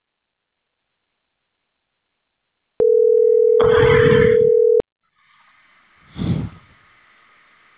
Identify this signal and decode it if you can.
Ringback tone (North America), 440 + 480 Hz.